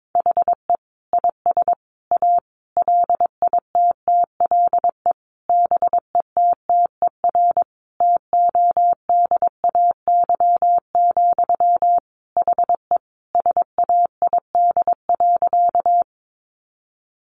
Morse code, heard as a keyed tone, about 22 wpm, 700 Hz. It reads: HE IS A LITTLE BETTER TODAY, HE SAID.
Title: Sentences from Top 600 Words 22wpm